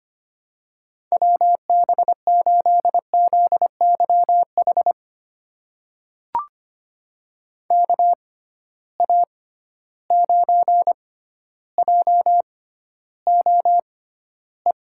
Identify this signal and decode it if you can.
Morse code, heard as a keyed tone, about 25 wpm, spaced as for 13 wpm, 700 Hz. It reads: WB8ZYH KA9JOE
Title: USA Call Signs - 6 Characters - 6x Character Spacing - 25wpm